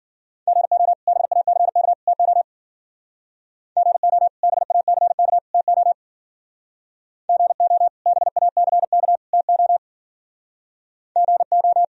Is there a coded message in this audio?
Yes: Morse code from 0.5 to 11.8 s, heard as a keyed tone, about 52 wpm, 700 Hz. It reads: GO BACK TO GO BACK TO GO BACK TO GO